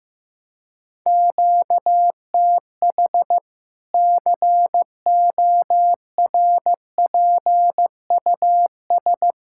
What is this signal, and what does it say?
Morse code, heard as a keyed tone, about 15 wpm, 700 Hz. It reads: QTH CORPUS
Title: Practice QSO - Exchange 1 - Beginning - 15wpm